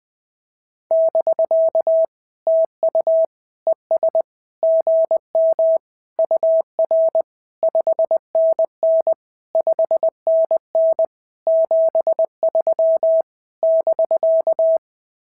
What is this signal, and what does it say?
Morse code, heard as a keyed tone, about 20 wpm, 650 Hz. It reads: <BK> TU ES GM UR 5NN 5NN 73 <BK>
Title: SOTA QSO Practice 20wpm